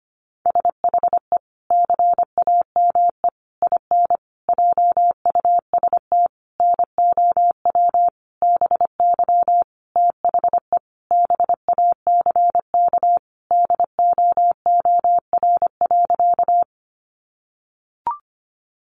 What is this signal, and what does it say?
Morse code, heard as a keyed tone, about 25 wpm, 700 Hz. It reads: SHE CAME IN JUST NOW BY THE BACK DOOR.
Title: Sentences from Top 300 Words 25wpm